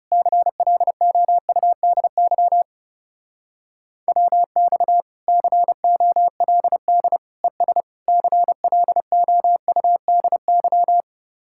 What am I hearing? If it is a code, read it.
Morse code, heard as a keyed tone, about 37 wpm, 700 Hz. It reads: CLOUDY WX COLD ES CLOUDY
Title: Speed-Racing Common QSO Phrases - 30wpm